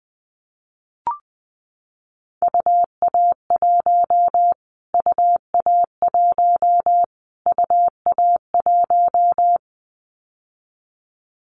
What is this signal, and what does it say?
Morse code, heard as a keyed tone, about 20 wpm, 700 Hz. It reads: UA1 UA1 UA1
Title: Callsign Prefixes for Top 25 DXCC - Repeated 3x - 20wpm